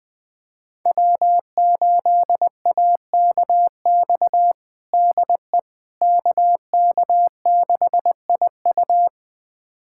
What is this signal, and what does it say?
Morse code, heard as a keyed tone, about 21 wpm, 700 Hz. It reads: W8AKX DE KK6IU
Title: USA Callsign DE Callsign - 5 Characters - 20wpm